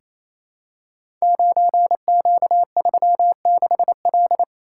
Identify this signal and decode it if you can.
Morse code, heard as a keyed tone, about 29 wpm, 700 Hz. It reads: 9Q36L
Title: Code Groups - 5 Letter:Numbers - 28wpm